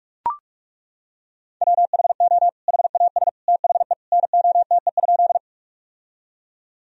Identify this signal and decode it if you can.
Morse code, heard as a keyed tone, about 41 wpm, 700 Hz. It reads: WHO HAS THE NOTE?
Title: Sentences from Top 1000 Words - Review Part 2 - 45wpm